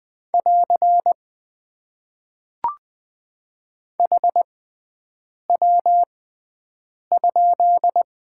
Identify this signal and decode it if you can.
Morse code, heard as a keyed tone, about 20 wpm, spaced as for 7 wpm, 700 Hz. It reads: <AR> HW?
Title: Top 90 QSO Elements - 6x Character Spacing - 20wpm